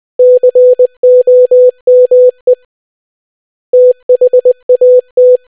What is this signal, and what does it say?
Morse code, heard as a keyed tone, about 20 wpm, 500 Hz. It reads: COME THAT